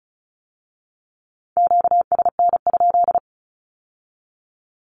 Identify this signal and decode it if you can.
Morse code, heard as a keyed tone, about 36 wpm, 700 Hz. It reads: QSN?
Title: Q-Codes with Spoken Meaning - 35wpm